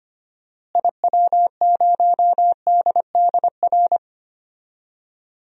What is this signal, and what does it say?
Morse code, heard as a keyed tone, about 25 wpm, 700 Hz. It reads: IW0DDR
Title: DXCC-100 Call Signs - 6 Characters - Repeated 3x - 25wpm